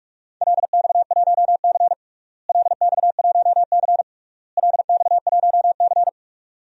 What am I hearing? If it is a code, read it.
Morse code, heard as a keyed tone, about 41 wpm, 700 Hz. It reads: LX1C LX1C LX1C
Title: DXCC-100 Call Signs - 4 Characters - Repeated 3x - 45wpm